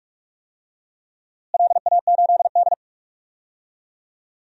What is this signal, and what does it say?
Morse code, heard as a keyed tone, about 46 wpm, 700 Hz. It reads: LA8D
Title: DXCC-100 Call Signs - 4 Characters - Repeated 2x - 45wpm